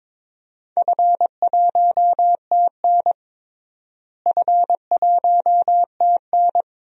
Morse code, heard as a keyed tone, about 22 wpm, 700 Hz. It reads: F1TN F1TN
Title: DXCC-100 Call Signs - 4 Characters - Repeated 2x - 22wpm